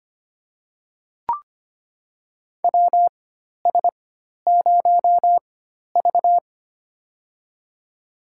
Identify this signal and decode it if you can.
Morse code, heard as a keyed tone, about 25 wpm, spaced as for 12 wpm, 700 Hz. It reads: WS0V
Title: USA Call Signs - 4 Characters - 4x Character Spacing - 25wpm